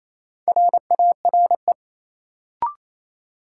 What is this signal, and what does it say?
Morse code, heard as a keyed tone, about 28 wpm, 700 Hz. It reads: RARE
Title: Sentences from Top 1100 Words - New Word Drill - 28wpm